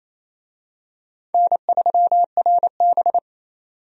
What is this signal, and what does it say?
Morse code, heard as a keyed tone, about 28 wpm, 700 Hz. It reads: N3RB
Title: USA Call Signs - 4 Characters - 28wpm